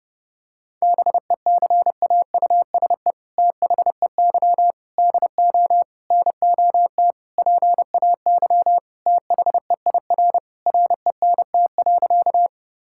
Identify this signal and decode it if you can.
Morse code, heard as a keyed tone, about 32 wpm, 700 Hz. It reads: BECAUSE THEY DO NOT PAY THEIR RENT.
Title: Sentences from Top 1000 Words - Review Part 5 - 30wpm